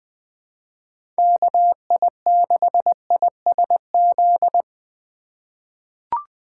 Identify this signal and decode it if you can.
Morse code, heard as a keyed tone, about 21 wpm, 700 Hz. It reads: KI6ISZ
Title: USA Call Signs - 6 Characters - 4x Character Spacing - 20wpm